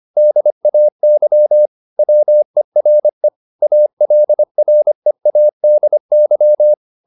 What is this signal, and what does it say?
Morse code, heard as a keyed tone, about 25 wpm, 600 Hz. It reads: DAY WERE ALREADY
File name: the old man and the sea part 1 25wpm